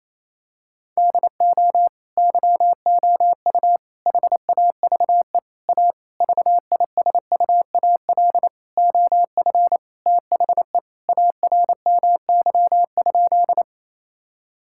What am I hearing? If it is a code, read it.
Morse code, heard as a keyed tone, about 28 wpm, 700 Hz. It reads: DO YOU HAVE A VISUAL OF THE ARMY?
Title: Sentences from Top 1000 Words - Review Part 6 - 28wpm